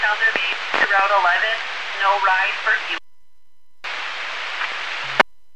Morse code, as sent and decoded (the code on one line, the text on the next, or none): none